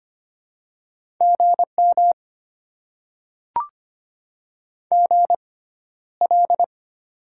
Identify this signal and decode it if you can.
Morse code, heard as a keyed tone, about 25 wpm, spaced as for 13 wpm, 690 Hz. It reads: GM GL
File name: Top 90 QSO Elements - 6x Character Spacing - 25wpm